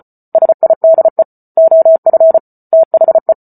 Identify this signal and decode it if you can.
Morse code, heard as a keyed tone, about 34 wpm, 650 Hz. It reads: SIDE OF THE
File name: Book Two / 08-Chapter 8 - 34 WPM